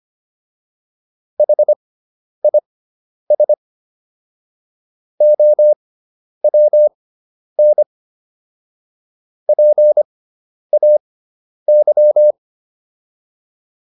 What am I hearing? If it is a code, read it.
Morse code, heard as a keyed tone, about 25 wpm, spaced as for 10 wpm, 600 Hz. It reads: HIS OWN PAY